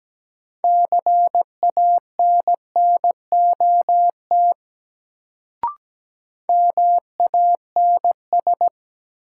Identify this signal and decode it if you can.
Morse code, heard as a keyed tone, about 17 wpm, 700 Hz. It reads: CANNOT MANS
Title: Sentences from Top 1000 Words - Review Part 6 - New Word Drill - 17wpm